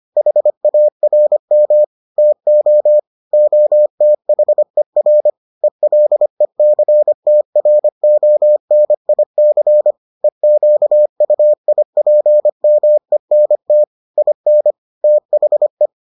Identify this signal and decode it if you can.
Morse code, heard as a keyed tone, about 25 wpm, 600 Hz. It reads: HARM TO OTHER ELECTRONIC EQUIPMENT IN THE